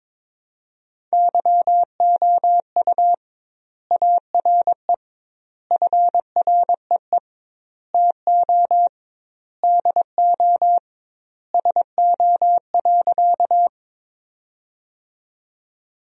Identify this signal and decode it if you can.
Morse code, heard as a keyed tone, about 22 wpm, 700 Hz. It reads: YOU ARE FREE TO DO SO.
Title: 2x Word-Spacing - Sentences from Top 500 Words - 22wpm